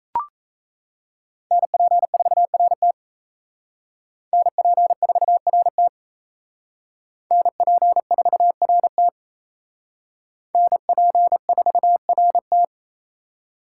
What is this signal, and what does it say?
Morse code, heard as a keyed tone, about 32 wpm, 700 Hz. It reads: NP4RT NP4RT NP4RT NP4RT
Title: Speed Racing - USA Call Signs 28wpm